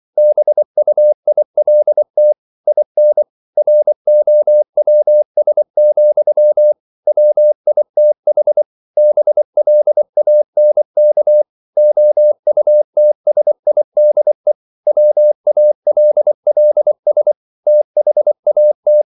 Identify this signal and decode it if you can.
Morse code, heard as a keyed tone, about 24 wpm, 600 Hz. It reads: BUILT IN ROWS, WITH BLANK OUTSIDE WALLS THAT